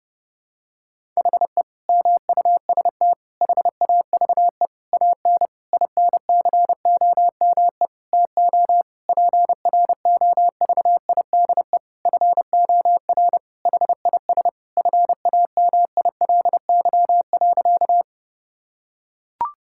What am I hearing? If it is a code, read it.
Morse code, heard as a keyed tone, about 30 wpm, 700 Hz. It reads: HE MUST HAVE AN INCOME TO PROVIDE FOR HIS FAMILY.
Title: Sentences from Top 1000 Words - Review Part 5 - 30wpm